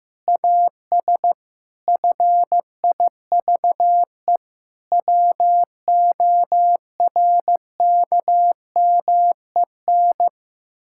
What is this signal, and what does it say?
Morse code, heard as a keyed tone, about 15 wpm, 700 Hz. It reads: AS FIVE WORKMEN
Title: Pangram Challenge v2 - 15wpm